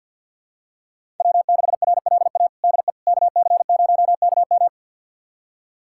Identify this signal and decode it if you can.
Morse code, heard as a keyed tone, about 51 wpm, 700 Hz. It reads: W6RLA DE KC0KM
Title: USA Callsign DE Callsign - 5 Characters - 50wpm